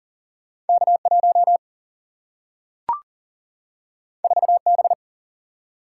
Morse code, heard as a keyed tone, about 40 wpm, 700 Hz. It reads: K 1 4 B
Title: Single-Letter Single-Number 40wpm